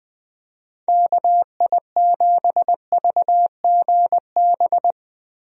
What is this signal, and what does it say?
Morse code, heard as a keyed tone, about 20 wpm, 700 Hz. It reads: KI7VGB
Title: USA Call Signs - 6 Characters - Repeated 2x - 20wpm